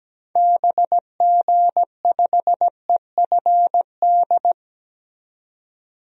Morse code, heard as a keyed tone, about 17 wpm, 700 Hz. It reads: BG5EFD
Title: DXCC-100 Call Signs - 6 Characters - 17wpm